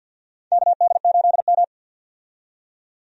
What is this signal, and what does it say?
Morse code, heard as a keyed tone, about 51 wpm, 700 Hz. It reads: KD8M
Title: USA Call Signs - 4 Characters - 50wpm